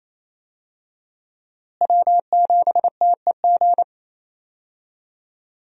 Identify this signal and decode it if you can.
Morse code, heard as a keyed tone, about 28 wpm, 700 Hz. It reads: W7TEG
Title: USA Call Signs - 5 Characters - Repeated 3x - 28wpm